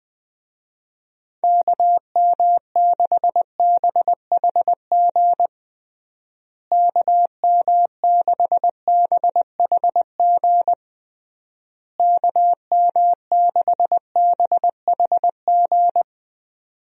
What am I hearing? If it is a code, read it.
Morse code, heard as a keyed tone, about 20 wpm, 700 Hz. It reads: KM6BHG KM6BHG KM6BHG
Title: USA Call Signs - 6 Characters - Repeated 3x - 20wpm